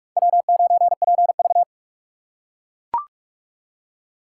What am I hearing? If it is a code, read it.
Morse code, heard as a keyed tone, about 38 wpm, 700 Hz. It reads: W9PV